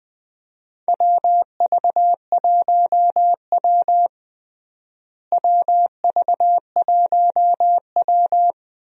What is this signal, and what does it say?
Morse code, heard as a keyed tone, about 20 wpm, 700 Hz. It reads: WV1W WV1W
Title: USA Call Signs - 4 Characters - Repeated 3x - 20wpm